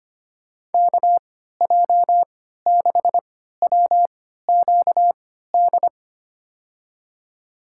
Morse code, heard as a keyed tone, about 25 wpm, spaced as for 14 wpm, 700 Hz. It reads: KJ6WQD
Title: USA Call Signs - 3x Character Spacing - 25wpm